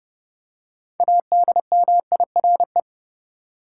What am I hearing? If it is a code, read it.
Morse code, heard as a keyed tone, about 32 wpm, 700 Hz. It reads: ADMIRE
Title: Six Letter Words - 30wpm